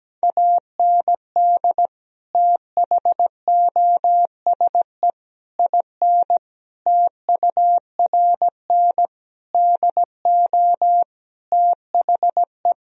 Morse code, heard as a keyed tone, about 17 wpm, 700 Hz. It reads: AND THOSE IN TURN DO THE
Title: Sentences from Top 1000 Words - Review Part 1 - 17wpm